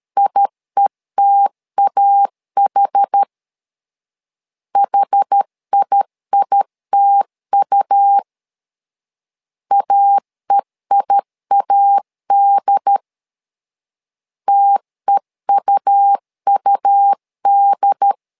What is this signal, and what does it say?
Morse code, heard as a keyed tone, about 13 wpm, 800 Hz. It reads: IETAH HIITU AEIAD TEUUD
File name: LEZIONI / 2 UHDN / 2 UHD 12wpm 1540